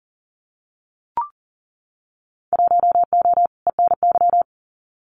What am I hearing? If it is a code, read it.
Morse code, heard as a keyed tone, about 42 wpm, 700 Hz. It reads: 1O ENY
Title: Winter Field Day Exchanges 40wpm